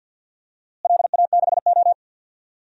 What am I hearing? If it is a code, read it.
Morse code, heard as a keyed tone, about 52 wpm, 700 Hz. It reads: LA6O
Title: DXCC-100 Call Signs - 4 Characters - Repeated 2x - 50wpm